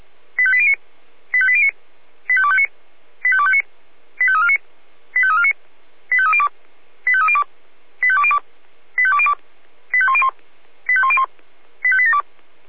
Five-tone selective call (ZVEI), heard as quick runs of five tones, tones of about 70 ms. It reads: 85809 85809 85259 85259 85349 85349 85302 85302 85202 85202 85101 85101 85872